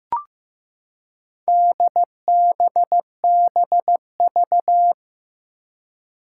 Morse code, heard as a keyed tone, about 15 wpm, 700 Hz. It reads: DBBV
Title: Four Challenging Letter-Number Combinations 15wpm